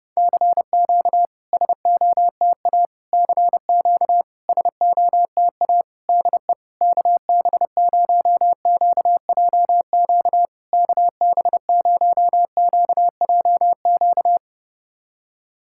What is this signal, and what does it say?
Morse code, heard as a keyed tone, about 30 wpm, 700 Hz. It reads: CQ SOTA CQ SOTA DE KB0QJQ KB0QJQ
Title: SOTA QSO Practice 30wpm